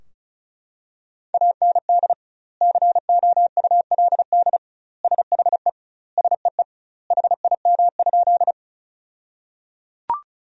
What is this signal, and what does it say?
Morse code, heard as a keyed tone, about 36 wpm, 700 Hz. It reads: AND COULD SHE SEE HIM?
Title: 2x Word-Spacing - Sentences from Top 100 Words - 35wpm